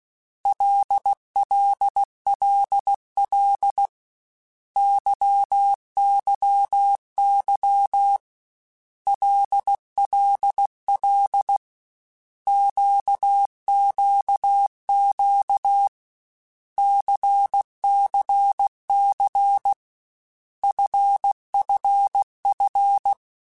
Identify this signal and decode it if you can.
Morse code, heard as a keyed tone, about 16 wpm, 800 Hz. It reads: LLLL YYY LLL QQQ CCC FFF
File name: Basic Letter Groups By7Groups / LFCYQ / LFCYQ@13